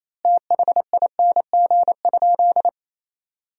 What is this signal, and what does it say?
Morse code, heard as a keyed tone, about 27 wpm, 700 Hz. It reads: THING?
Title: 2x Word-Spacing - Sentences from Top 200 Words - 28wpm